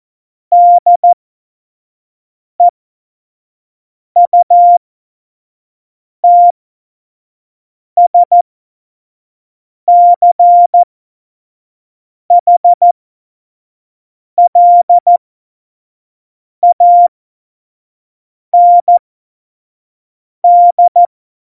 Morse code, heard as a keyed tone, about 14 wpm, spaced as for 5 wpm, 700 Hz. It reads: DEUTSCHLAND
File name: Morsekurs / Witze / Witz 11 Juli 17, Tempo 25 F70